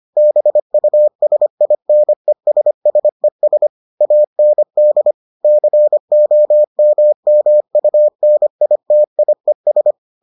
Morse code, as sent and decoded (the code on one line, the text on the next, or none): -... ..- ... .. -. ... ... ... / .- -. -.. / -.-. --- -- -- ..- -. .. - .. ...
BUSINESSES AND COMMUNITIES